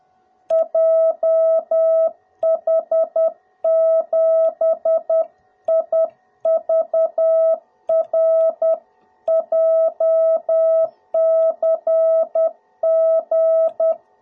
Morse code, heard as a keyed tone, about 10 wpm, 660 Hz. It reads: JH7IVR JCG
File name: jh8cbh / 2021cw / cw02b